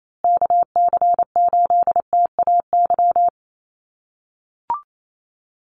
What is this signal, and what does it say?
Morse code, heard as a keyed tone, about 28 wpm, 700 Hz. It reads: KC8TAY
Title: USA Call Signs Repeated 2x 28wpm